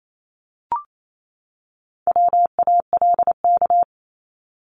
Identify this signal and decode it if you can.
Morse code, heard as a keyed tone, about 28 wpm, 700 Hz. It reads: WALK
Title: ING Suffix - Top 100 Words with Base Words 28wpm